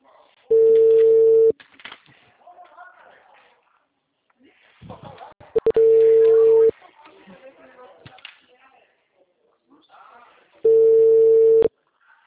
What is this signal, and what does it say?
Ringback tone (China, 450 Hz), with speech.
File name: out- -2011